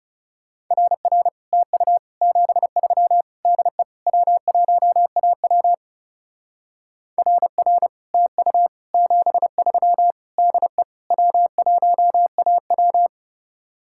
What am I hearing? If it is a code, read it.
Morse code, heard as a keyed tone, about 36 wpm, 700 Hz. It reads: RR TU 73 DE W1AW RR TU 73 DE W1AW
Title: Speed-Racing Common QSO Phrases Encore - 30wpm